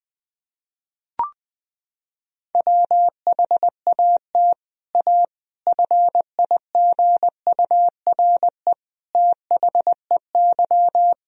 Morse code, heard as a keyed tone, about 21 wpm, 700 Hz. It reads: WHAT A FIGURE THEY
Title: Sentences from Top 1000 Words - Review Part 2 - 20wpm